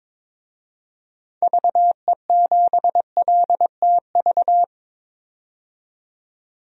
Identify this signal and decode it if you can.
Morse code, heard as a keyed tone, about 22 wpm, 700 Hz. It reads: VE7LTV
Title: DXCC-100 Call Signs - 6 Characters - 22wpm